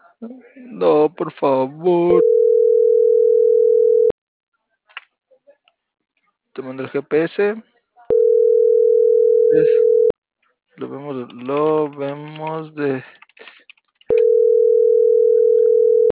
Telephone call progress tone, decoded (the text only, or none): ringback tone